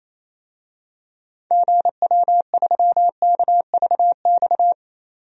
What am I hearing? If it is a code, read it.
Morse code, heard as a keyed tone, about 28 wpm, 700 Hz. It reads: GW3KVX
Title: DXCC-100 Call Signs - 6 Characters - 28wpm